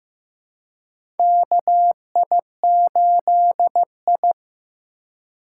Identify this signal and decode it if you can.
Morse code, heard as a keyed tone, about 15 wpm, 700 Hz. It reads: KI8I
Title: K1USN SST QSO Practice - 15wpm